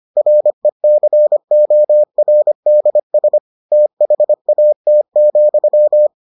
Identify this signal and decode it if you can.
Morse code, heard as a keyed tone, about 25 wpm, 600 Hz. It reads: RECORDS THAT,